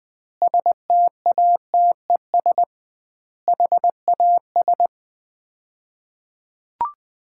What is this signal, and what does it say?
Morse code, heard as a keyed tone, about 20 wpm, 700 Hz. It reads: STATES HAS
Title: 2x Word-Spacing - Sets of 4 Words - 20wpm